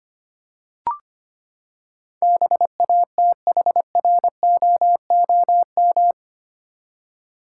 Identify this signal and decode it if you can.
Morse code, heard as a keyed tone, about 25 wpm, 700 Hz. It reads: BATHROOM
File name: Eight Letter Words - 25wpm